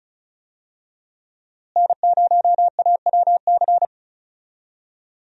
Morse code, heard as a keyed tone, about 34 wpm, 700 Hz. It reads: N0AWC
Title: USA Call Signs - 5 Characters - Repeated 2x - 35wpm